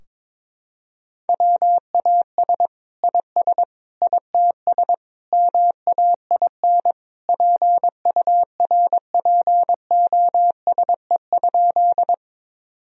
Morse code, heard as a keyed tone, about 22 wpm, 700 Hz. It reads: WAS IS ITS MAIN PURPOSE?
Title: Sentences from Top 1000 Words - Review Part 2 - 22wpm